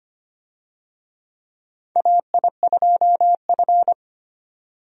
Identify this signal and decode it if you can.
Morse code, heard as a keyed tone, about 25 wpm, 700 Hz. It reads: AI2F
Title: USA Call Signs - 4 Characters - 25wpm